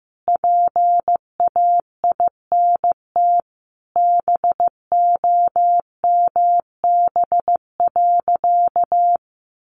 Morse code, heard as a keyed tone, about 15 wpm, 700 Hz. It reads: PAINT BOMB.
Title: A Sound Of Thunder 15wpm